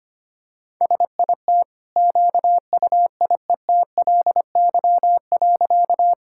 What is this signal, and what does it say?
Morse code, heard as a keyed tone, about 25 wpm, 700 Hz. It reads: SIT QUIETLY.
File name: Easy Sentences with Hints - 25wpm